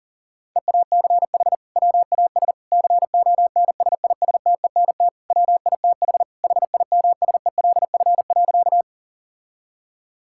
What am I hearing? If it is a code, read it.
Morse code, heard as a keyed tone, about 40 wpm, 700 Hz. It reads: EACH WAS CONSISTENT WITH HIMSELF.
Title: Sentences from Top 1600 Words 40wpm